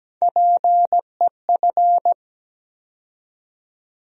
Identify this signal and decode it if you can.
Morse code, heard as a keyed tone, about 17 wpm, 700 Hz. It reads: PEF
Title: USA Call Sign Postfixes - 3 Letters 17wpm